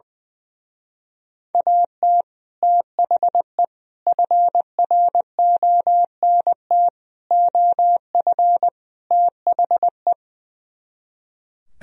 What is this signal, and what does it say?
Morse code, heard as a keyed tone, about 20 wpm, 700 Hz. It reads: AT THE FRONT OF THE
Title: Speed-Racing - Sets of 5 Words 20wpm